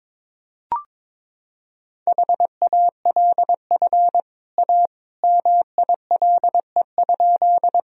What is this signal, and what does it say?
Morse code, heard as a keyed tone, about 22 wpm, 700 Hz. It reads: HALF A MILE?
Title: Sentences from Top 1000 Words - Review Part 1 - 22wpm